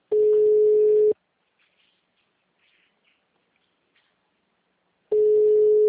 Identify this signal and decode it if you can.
Ringback tone (Europe, 425 Hz).
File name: out- -2017- - - .60802